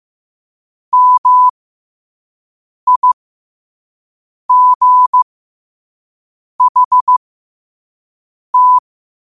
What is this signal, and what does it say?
Morse code, heard as a keyed tone, about 15 wpm, spaced as for 5 wpm, 1000 Hz. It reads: MIGHT
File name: qotdfiles / 5wpm